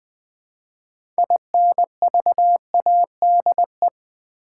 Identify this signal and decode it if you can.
Morse code, heard as a keyed tone, about 20 wpm, 700 Hz. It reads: INVADE
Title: Build A Word - 3 Letters At A Time - 6-Letter Words - 20wpm